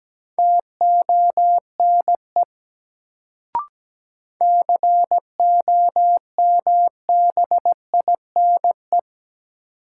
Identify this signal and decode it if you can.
Morse code, heard as a keyed tone, about 17 wpm, 700 Hz. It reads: TONE COMBINE
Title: Sentences from Top 1600 Words - New Word Drill - 17wpm